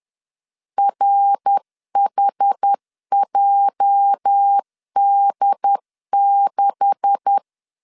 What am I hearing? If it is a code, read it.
Morse code, heard as a keyed tone, about 10 wpm, 800 Hz. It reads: RHJD6
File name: LEZIONI / 9 286 10wpm 2735